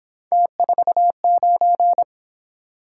Morse code, heard as a keyed tone, about 26 wpm, 700 Hz. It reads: T49
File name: Speed-Racing - Three Letter-Number Combinations - 22wpm